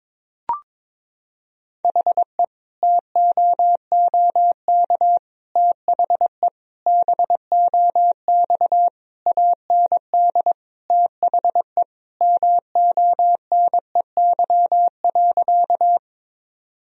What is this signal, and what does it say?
Morse code, heard as a keyed tone, about 22 wpm, 700 Hz. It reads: HE TOOK THE BOX AND THE MONEY.
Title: Sentences from Top 1000 Words - Review Part 2 - 22wpm